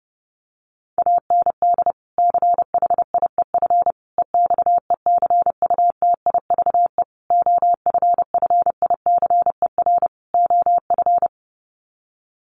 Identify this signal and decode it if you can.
Morse code, heard as a keyed tone, about 30 wpm, 700 Hz. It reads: AND CHIEF EXECUTIVE OFFICER OF
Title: Sets of 5 Words Encore2 30wpm